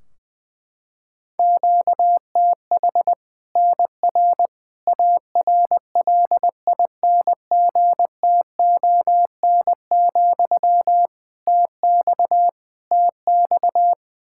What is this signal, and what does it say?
Morse code, heard as a keyed tone, about 20 wpm, 700 Hz. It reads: QTH NR ARLINGTON, TX TX
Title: Practice QSO - Exchange 1 - Beginning - 20wpm